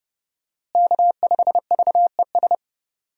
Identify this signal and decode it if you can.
Morse code, heard as a keyed tone, about 30 wpm, 700 Hz. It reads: K5VES
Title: USA Call Signs - 5 Characters - Repeated 3x - 30wpm